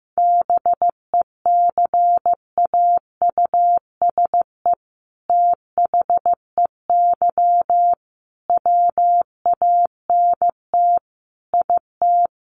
Morse code, heard as a keyed tone, about 15 wpm, 700 Hz. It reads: BECAUSE THEY WANT IT
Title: Sentences from Top 800 Words 15wpm